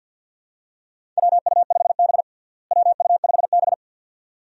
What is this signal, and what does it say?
Morse code, heard as a keyed tone, about 51 wpm, 700 Hz. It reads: WU5B WU5B
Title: USA Call Signs - 4 Characters - Repeated 2x - 50wpm